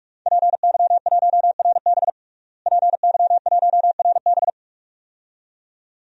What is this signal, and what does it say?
Morse code, heard as a keyed tone, about 46 wpm, 700 Hz. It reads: PY1RB PY1RB
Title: DXCC-100 Call Signs - 5 Characters - Repeated 2x - 45wpm